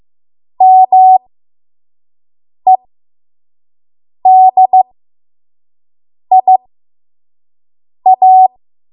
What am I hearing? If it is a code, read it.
Morse code, heard as a keyed tone, about 15 wpm, spaced as for 5 wpm, 750 Hz. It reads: MEDIA
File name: Morse / 05WPM